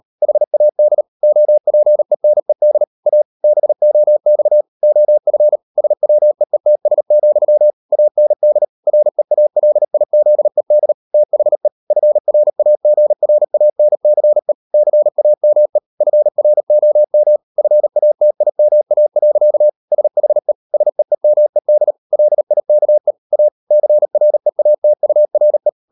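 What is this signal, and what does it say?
Morse code, heard as a keyed tone, about 38 wpm, 600 Hz. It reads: HAD OPENED A BOX OF SWEETS, AND REALIZED THE FRAGRANCE CAME FROM FATIMA. SHE SEEMED LIKE A CREATURE
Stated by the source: Morse code practice at 38 WPM, sending Ken Follett's novel Never in English, from CW med Gnister.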